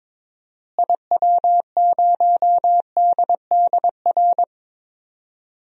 Morse code, heard as a keyed tone, about 22 wpm, 700 Hz. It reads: IW0DDR
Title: DXCC-100 Call Signs - 6 Characters - Repeated 3x - 22wpm